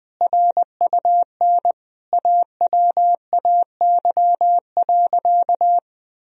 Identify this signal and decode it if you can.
Morse code, heard as a keyed tone, about 21 wpm, 700 Hz. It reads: RUN AWAY.
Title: Sentences from Top 1000 Words - Review Part 1 - 20wpm